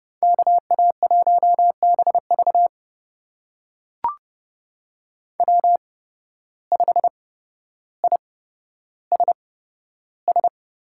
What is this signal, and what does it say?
Morse code, heard as a keyed tone, about 29 wpm, spaced as for 13 wpm, 700 Hz. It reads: KA1BV W5ISS
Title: USA Call Signs - 8x Character Spacing - 30wpm